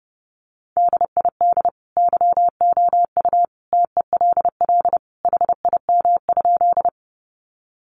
Morse code, heard as a keyed tone, about 30 wpm, 700 Hz. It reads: DID YOU TELL HIM?
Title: Sentences from Top 200 Words 30wpm